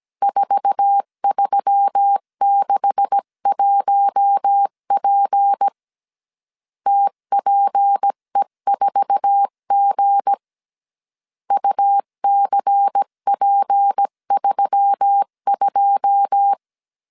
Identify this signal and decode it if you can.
Morse code, heard as a keyed tone, about 17 wpm, 800 Hz. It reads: 4361P TPE4G UCP32
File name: CW / DET / 1 9 1500C S B 16wpm 2856